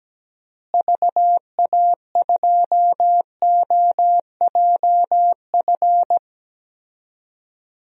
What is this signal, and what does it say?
Morse code, heard as a keyed tone, about 17 wpm, 700 Hz. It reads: VA2OJF
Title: DXCC-100 Call Signs - 6 Characters - Repeated 3x - 17wpm